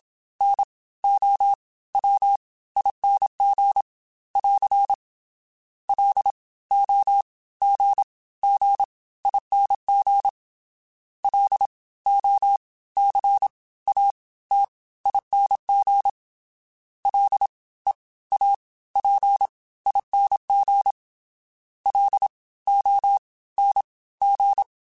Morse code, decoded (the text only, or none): NOWING <AR> LOGGING LOCATING LEAPING LONG